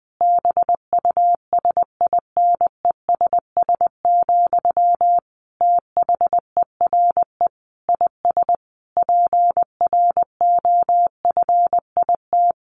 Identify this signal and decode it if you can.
Morse code, heard as a keyed tone, about 20 wpm, 700 Hz. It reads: BUSINESS, THERE IS PROFIT
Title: Sentences from Top 1400 Words 20wpm